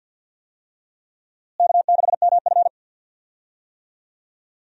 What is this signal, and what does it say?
Morse code, heard as a keyed tone, about 51 wpm, 700 Hz. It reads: K6MF
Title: USA Call Signs - 4 Characters - Repeated 3x - 50wpm